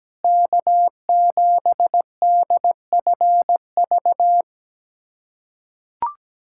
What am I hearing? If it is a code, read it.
Morse code, heard as a keyed tone, about 17 wpm, 700 Hz. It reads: K7DFV
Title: USA Callsign DE Callsign - 5 Characters - 17wpm